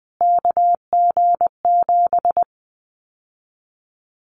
Morse code, heard as a keyed tone, about 20 wpm, 700 Hz. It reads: KG7
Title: Callsign Prefixes for Top 25 DXCC - Repeated 3x - 20wpm